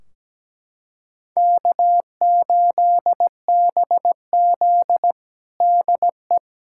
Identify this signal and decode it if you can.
Morse code, heard as a keyed tone, about 17 wpm, 700 Hz. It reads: K8BZ DE
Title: Practice QSO - Exchange 4 - Ending - 17wpm